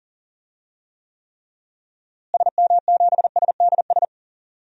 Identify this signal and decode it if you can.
Morse code, heard as a keyed tone, about 40 wpm, 700 Hz. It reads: SM7SDS
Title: DXCC-100 Call Signs - 6 Characters - Repeated 3x - 40wpm